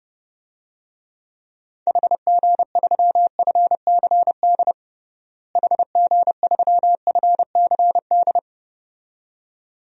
Morse code, heard as a keyed tone, about 29 wpm, 700 Hz. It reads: HG3FCD HG3FCD
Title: DXCC-100 Call Signs - 6 Characters - Repeated 2x - 30wpm